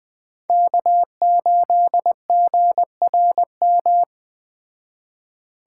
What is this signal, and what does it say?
Morse code, heard as a keyed tone, about 20 wpm, 700 Hz. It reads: K8GRM
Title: Speed Racing - USA 1x3 Call Signs 17wpm